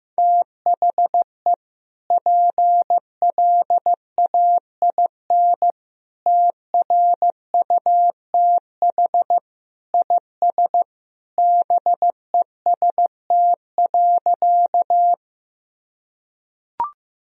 Morse code, heard as a keyed tone, about 15 wpm, 700 Hz. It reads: THE PLAIN TRUTH IS BEST.
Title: Sentences from Top 1000 Words - Review Part 2 - 15wpm